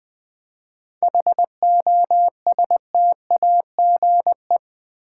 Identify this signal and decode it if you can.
Morse code, heard as a keyed tone, about 20 wpm, 690 Hz. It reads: HOSTAGE
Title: 7 Letter Words - First 3 Letters Given - 20wpm